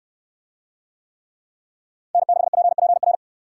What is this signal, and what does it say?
Morse code, heard as a keyed tone, about 64 wpm, 700 Hz. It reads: N6PPR